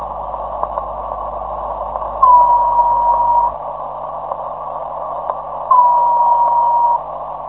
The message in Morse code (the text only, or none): none